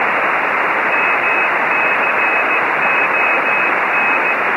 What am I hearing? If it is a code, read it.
Morse code, heard as a keyed tone, about 15 wpm, 2550 Hz, with radio noise. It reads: MCON